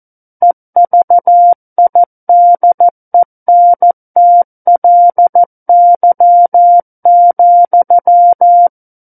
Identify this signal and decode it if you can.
Morse code, heard as a keyed tone, about 14 wpm, 700 Hz. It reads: EVIDENTLY,
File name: Book One / 05-Chapter 5 - 14 WPM